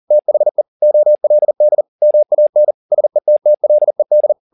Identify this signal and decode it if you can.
Morse code, heard as a keyed tone, about 40 wpm, 600 Hz. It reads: THE OLD MAN SETTLED
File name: the old man and the sea 2 40wpm